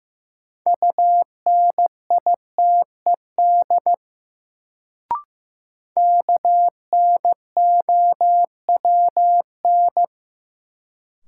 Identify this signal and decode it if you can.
Morse code, heard as a keyed tone, about 15 wpm, 700 Hz. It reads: UNITED KNOWN
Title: Sentences from Top 600 Words - New Word Drill - 15wpm